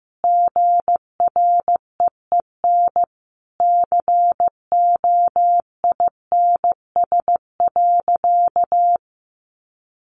Morse code, heard as a keyed tone, about 14 wpm, 700 Hz. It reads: GREEN COINS.
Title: A Sound Of Thunder 15wpm